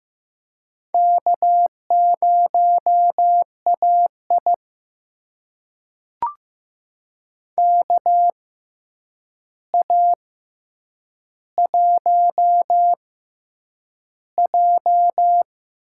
Morse code, heard as a keyed tone, about 15 wpm, spaced as for 8 wpm, 700 Hz. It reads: K0AI KA1J
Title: USA Call Signs - 4 Characters - 6x Character Spacing - 15wpm